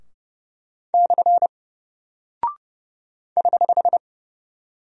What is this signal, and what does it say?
Morse code, heard as a keyed tone, about 31 wpm, 700 Hz. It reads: / <HH>
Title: Single Character (Letter:Number:Punctuation:Prosign) Alt 30wpm